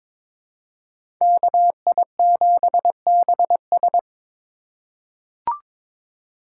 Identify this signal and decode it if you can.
Morse code, heard as a keyed tone, about 22 wpm, 700 Hz. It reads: KI7BS
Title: USA Call Signs - 5 Characters - 2x Character Spacing - 22wpm